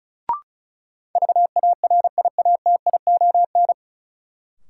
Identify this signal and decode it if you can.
Morse code, heard as a keyed tone, about 36 wpm, 700 Hz. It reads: VARIATION
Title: Sentences from Top 1600 Words - New Word Drill Alt - 35wpm